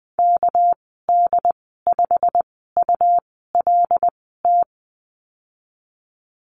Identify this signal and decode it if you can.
Morse code, heard as a keyed tone, about 19 wpm, spaced as for 14 wpm, 700 Hz. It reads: KD5ULT
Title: 213 Lesson 34g - Number 8 - Call Signs 20wpm